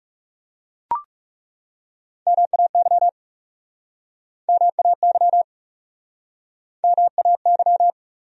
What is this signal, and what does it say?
Morse code, heard as a keyed tone, about 41 wpm, 700 Hz. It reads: MAY MAY MAY